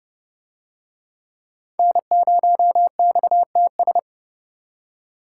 Morse code, heard as a keyed tone, about 28 wpm, 700 Hz. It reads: N0XTS